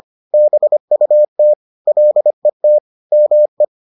Morse code, heard as a keyed tone, about 25 wpm, 600 Hz. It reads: BUT LET ME